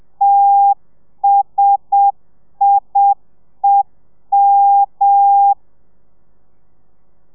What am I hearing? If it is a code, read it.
Morse code, heard as a keyed tone, about 7 wpm, 800 Hz. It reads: TSIEM
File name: telegrafia / i2tfj / Lez1 30